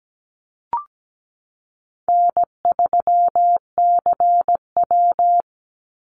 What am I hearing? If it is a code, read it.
Morse code, heard as a keyed tone, about 17 wpm, 700 Hz. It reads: N3CW
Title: Easy Call Signs - 17wpm